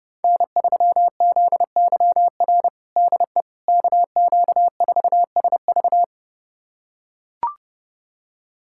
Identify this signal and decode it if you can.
Morse code, heard as a keyed tone, about 30 wpm, 700 Hz. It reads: N3ZYR DE KQ4SV
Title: USA Callsign DE Callsign - 5 Characters - 30wpm